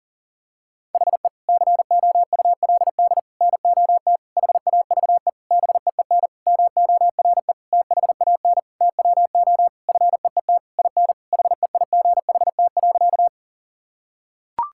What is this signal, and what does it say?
Morse code, heard as a keyed tone, about 40 wpm, 700 Hz. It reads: HE COULD NOT HAVE BEEN MORE THAN TWO FEET IN HEIGHT.
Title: Sentences from Top 1600 Words 40wpm